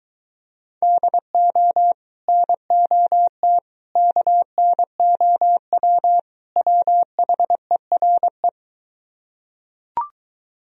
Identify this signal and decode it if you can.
Morse code, heard as a keyed tone, about 23 wpm, 700 Hz. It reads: DO NOT KNOW WHERE